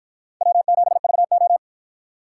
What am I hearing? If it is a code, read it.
Morse code, heard as a keyed tone, about 46 wpm, 700 Hz. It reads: W7VO